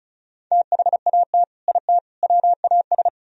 Morse code, heard as a keyed tone, about 44 wpm, 700 Hz. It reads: THAT IT WAS
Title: Speed-Racing - Sets of 3 Words 35wpm